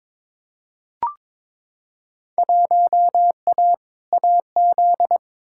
Morse code, heard as a keyed tone, about 22 wpm, 700 Hz. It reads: 1A AZ
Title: Field Day Exchanges - 22wpm